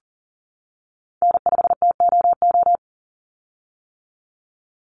Morse code, heard as a keyed tone, about 40 wpm, 700 Hz. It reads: N5TOO
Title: Call Signs - 40wpm